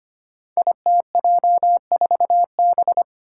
Morse code, heard as a keyed tone, about 25 wpm, 700 Hz. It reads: ITJ4B
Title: Five Challenging Letter-Number Combinations 25wpm